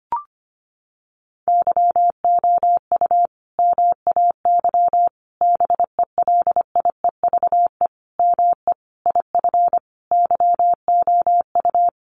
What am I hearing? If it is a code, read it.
Morse code, heard as a keyed tone, about 25 wpm, 700 Hz. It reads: YOU MAY BELIEVE ME IF YOU
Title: Sentences from Top 1000 Words 25wpm